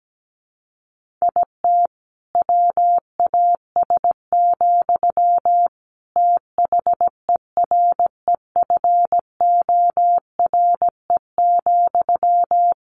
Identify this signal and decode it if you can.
Morse code, heard as a keyed tone, about 17 wpm, 700 Hz. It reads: IT WAS, THEREFORE,
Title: Sentences from Top 1500 Words 17wpm